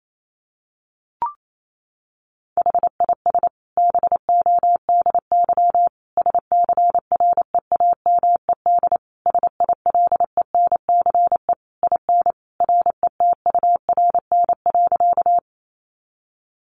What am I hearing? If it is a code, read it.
Morse code, heard as a keyed tone, about 28 wpm, 700 Hz. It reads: HIS BODY SCREAMED SILENCE IN RETURN.
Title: A Sound Of Thunder 28wpm